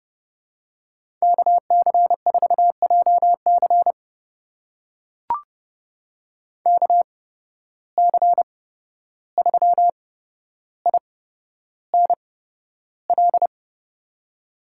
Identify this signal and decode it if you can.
Morse code, heard as a keyed tone, about 30 wpm, spaced as for 12 wpm, 700 Hz. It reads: KC4JC KC3INL
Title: USA Call Signs - 8x Character Spacing - 30wpm